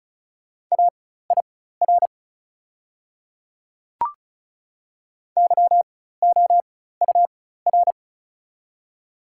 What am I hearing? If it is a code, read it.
Morse code, heard as a keyed tone, about 34 wpm, spaced as for 16 wpm, 700 Hz. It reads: AIR YOUR
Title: 4x Character Spacing - Top 200 Words - 35wpm